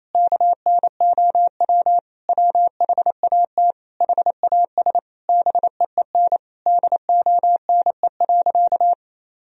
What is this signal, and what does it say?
Morse code, heard as a keyed tone, about 28 wpm, 700 Hz. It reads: KNOW WHAT HAS BEEN DONE.
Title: Sentences from Top 1000 Words - Review Part 1 - 28wpm